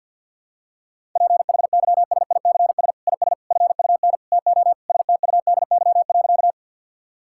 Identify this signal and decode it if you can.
Morse code, heard as a keyed tone, about 52 wpm, 700 Hz. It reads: PHYSICS IS FUN TO STUDY.
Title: Sentences from Top 1000 Words - Review Part 5 - 50wpm